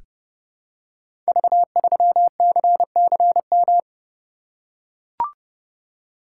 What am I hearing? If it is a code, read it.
Morse code, heard as a keyed tone, about 30 wpm, 700 Hz. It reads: V3CCM
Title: Code Groups - 5 Letter:Numbers - 30wpm